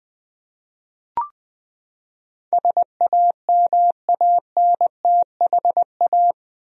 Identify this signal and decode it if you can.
Morse code, heard as a keyed tone, about 20 wpm, 700 Hz. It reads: SAMANTHA
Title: USA Common Names - 20wpm